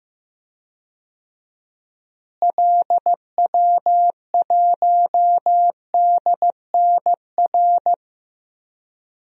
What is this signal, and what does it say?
Morse code, heard as a keyed tone, about 15 wpm, 700 Hz. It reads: LW1DNR
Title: DXCC-100 Call Signs - 6 Characters - Repeated 2x - 15wpm